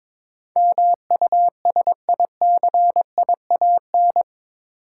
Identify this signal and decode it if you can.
Morse code, heard as a keyed tone, about 21 wpm, 700 Hz. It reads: MUSICIAN
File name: practice / 22wpm / Eight Letter Words - 22wpm